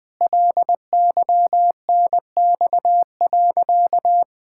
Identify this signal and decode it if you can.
Morse code, heard as a keyed tone, about 20 wpm, 700 Hz. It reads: LYNX.
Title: Pangram Challenge v2 - 20wpm